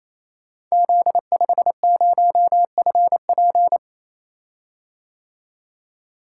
Morse code, heard as a keyed tone, about 29 wpm, 700 Hz. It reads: Z50FP
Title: Code Groups - 5 Letter:Numbers - 28wpm